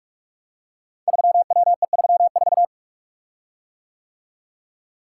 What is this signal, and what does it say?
Morse code, heard as a keyed tone, about 44 wpm, 700 Hz. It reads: 3WE34
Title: Code Groups - 5 Letter:Numbers - 45wpm